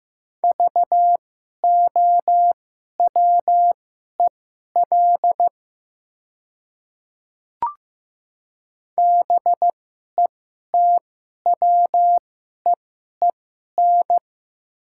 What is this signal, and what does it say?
Morse code, heard as a keyed tone, about 15 wpm, spaced as for 11 wpm, 700 Hz. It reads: VOWEL BETWEEN
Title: 2x Character Spacing - Top 500 Words - 15wpm